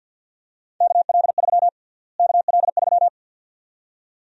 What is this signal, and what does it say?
Morse code, heard as a keyed tone, about 50 wpm, 700 Hz. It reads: KL3 KL3
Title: USA Call Sign Prefixes - Repeated 2x - 50wpm